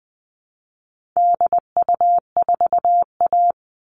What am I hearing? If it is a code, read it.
Morse code, heard as a keyed tone, about 19 wpm, 700 Hz. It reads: DU4A
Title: All Cut Numbers - 4-Digit 20wpm